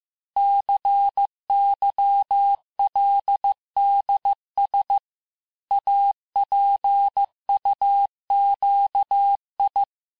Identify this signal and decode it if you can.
Morse code, heard as a keyed tone, about 15 wpm, 800 Hz. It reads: CYLDS APUQI
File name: cours-du-lundi files / cours-1585-texte-chiffre-vitesse-14mpm